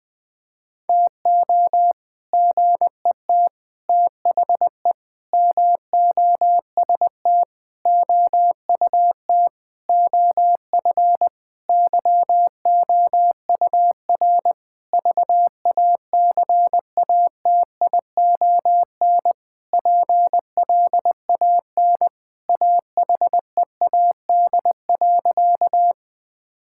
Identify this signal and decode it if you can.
Morse code, heard as a keyed tone, about 21 wpm, 700 Hz. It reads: TO GET THE MOST OUT OF YOUR VACATION PLAN AHEAD.
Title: Sentences with 7-Word Phrases - 20wpm